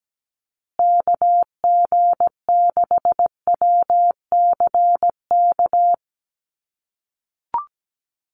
Morse code, heard as a keyed tone, about 17 wpm, 700 Hz. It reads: KG6WCK
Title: USA Call Signs Repeated 3x 17wpm